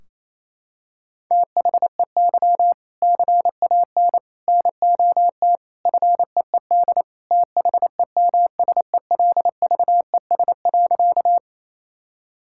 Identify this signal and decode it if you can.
Morse code, heard as a keyed tone, about 28 wpm, 700 Hz. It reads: THEY CAN NOT FEED THEMSELVES.
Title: Sentences from Top 1600 Words 28wpm